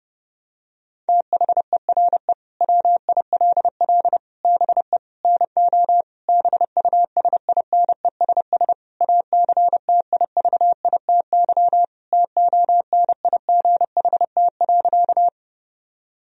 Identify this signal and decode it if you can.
Morse code, heard as a keyed tone, about 30 wpm, 700 Hz. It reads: THERE WILL BE NO BUSINESS ACTIVITY TONIGHT.
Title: Sentences from Top 1000 Words - Review Part 4 - 30wpm